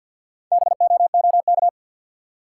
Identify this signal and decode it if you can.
Morse code, heard as a keyed tone, about 53 wpm, 700 Hz. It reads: BOOK
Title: Four Letter Words - 50wpm